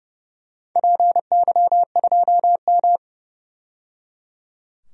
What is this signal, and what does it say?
Morse code, heard as a keyed tone, about 30 wpm, 700 Hz. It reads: PY2M
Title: DXCC-100 Call Signs - 4 Characters - 30wpm